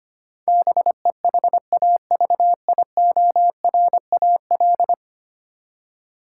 Morse code, heard as a keyed tone, about 25 wpm, 700 Hz. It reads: BEHAVIORAL
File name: Ten Letter Words - 25wpm